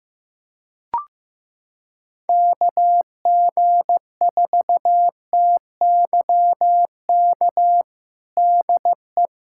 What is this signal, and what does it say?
Morse code, heard as a keyed tone, about 15 wpm, 700 Hz. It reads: KG4TYK DE
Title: USA Callsign DE Callsign - 6 Characters - 15wpm